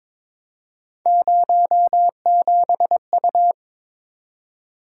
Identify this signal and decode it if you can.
Morse code, heard as a keyed tone, about 22 wpm, 700 Hz. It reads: 07U
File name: practice / 22wpm / Three Letter-Number Combinations - 22wpm